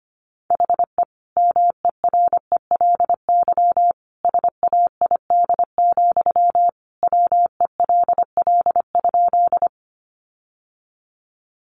Morse code, heard as a keyed tone, about 25 wpm, 700 Hz. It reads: HE MERELY SAID, WELL?
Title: Sentences from Top 1400 Words 25wpm